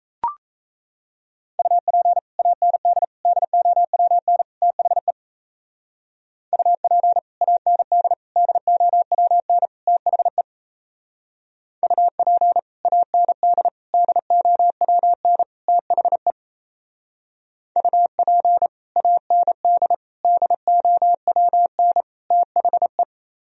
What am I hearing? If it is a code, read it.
Morse code, heard as a keyed tone, about 37 wpm, 700 Hz. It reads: UP AND DOWN THE UP AND DOWN THE UP AND DOWN THE UP AND DOWN THE